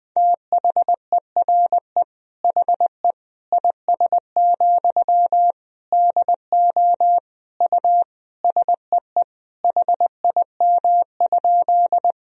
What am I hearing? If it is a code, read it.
Morse code, heard as a keyed tone, about 20 wpm, 700 Hz. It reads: THERE HE IS, DO U SEE HIM?
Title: Sentences from Top 100 Words and Abbreviations 20wpm